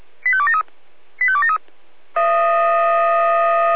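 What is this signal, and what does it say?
Five-tone selective call (ZVEI), heard as quick runs of five tones, tones of about 70 ms. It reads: 85273 85273
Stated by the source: ZVEI five-tone call received by SDR, recorded by feuerwehr-orlamuende.org (fire brigade Orlamünde).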